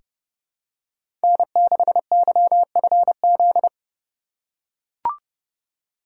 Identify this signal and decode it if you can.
Morse code, heard as a keyed tone, about 30 wpm, 700 Hz. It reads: N6YFZ
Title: USA Call Signs - 5 Characters - 4x Character Spacing - 30wpm